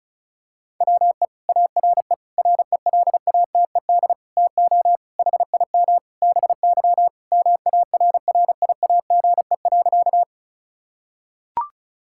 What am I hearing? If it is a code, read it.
Morse code, heard as a keyed tone, about 34 wpm, 700 Hz. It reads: WE ARE RELATED TO HIM BY MARRIAGE.
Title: Sentences from Top 1000 Words - Review Part 5 - 35wpm